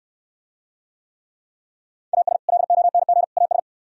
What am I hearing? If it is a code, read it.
Morse code, heard as a keyed tone, about 69 wpm, 700 Hz. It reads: AS LONG AS